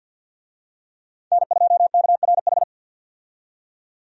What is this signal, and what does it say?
Morse code, heard as a keyed tone, about 46 wpm, 700 Hz. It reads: N2KRH
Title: USA Call Signs - 5 Characters - 50wpm